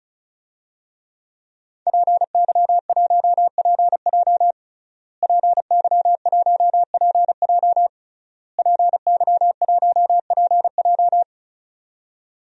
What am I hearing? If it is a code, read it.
Morse code, heard as a keyed tone, about 36 wpm, 700 Hz. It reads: PY1PJ PY1PJ PY1PJ
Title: DXCC-100 Call Signs - 5 Characters - Repeated 3x - 35wpm